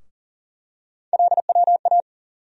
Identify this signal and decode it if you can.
Morse code, heard as a keyed tone, about 40 wpm, 700 Hz. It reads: LWA
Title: USA Call Sign Postfixes - 3 Letters 40wpm